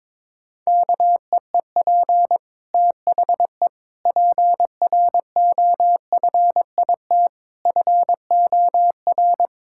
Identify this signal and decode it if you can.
Morse code, heard as a keyed tone, about 22 wpm, 700 Hz. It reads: KEEP THE PROFIT FOR
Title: Sentences from Top 1000 Words - Review Part 5 - 22wpm